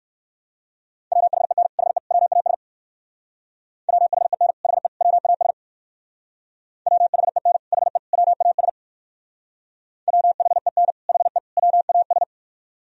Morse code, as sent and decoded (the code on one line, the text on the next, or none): .-- .... -. / .... / .-- .- ... / .-- .... -. / .... / .-- .- ... / .-- .... -. / .... / .-- .- ... / .-- .... -. / .... / .-- .- ...
WHEN HE WAS WHEN HE WAS WHEN HE WAS WHEN HE WAS